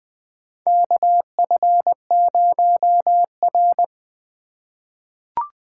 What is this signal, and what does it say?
Morse code, heard as a keyed tone, about 20 wpm, 700 Hz. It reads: KF0R
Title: USA Call Signs - 4 Characters - 4x Character Spacing - 20wpm